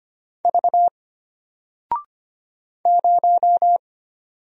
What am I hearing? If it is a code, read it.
Morse code, heard as a keyed tone, about 25 wpm, 700 Hz. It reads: V 0
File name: Single-Letter-Number-25wpm